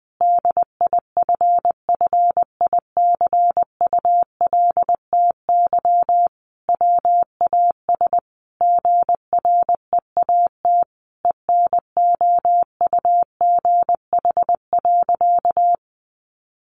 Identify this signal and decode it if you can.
Morse code, heard as a keyed tone, about 20 wpm, 700 Hz. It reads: DIFFICULTY WAS GREAT ENOUGH.
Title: Sentences from Top 1500 Words 20wpm